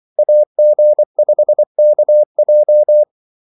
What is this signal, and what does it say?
Morse code, heard as a keyed tone, about 24 wpm, 600 Hz. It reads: AG5KJ
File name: Callsigns-24wpm